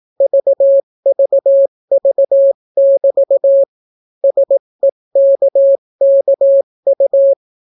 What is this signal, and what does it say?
Morse code, heard as a keyed tone, about 18 wpm, 550 Hz. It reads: VVV<BT> SEKKU